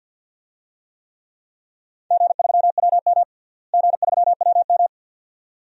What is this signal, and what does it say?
Morse code, heard as a keyed tone, about 47 wpm, 700 Hz. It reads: G3WM G3WM
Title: DXCC-100 Call Signs - 4 Characters - Repeated 2x - 50wpm